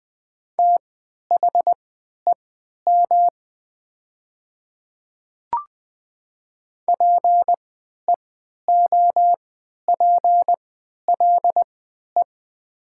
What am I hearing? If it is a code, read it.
Morse code, heard as a keyed tone, about 20 wpm, spaced as for 11 wpm, 700 Hz. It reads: THEM PEOPLE
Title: 3x Character Spacing - Top 100 Words - 20wpm